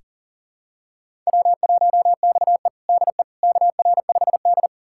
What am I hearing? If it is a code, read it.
Morse code, heard as a keyed tone, about 40 wpm, 700 Hz. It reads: W1XE DE KR5D
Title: USA Callsign DE Callsign - 4 Characters - 40wpm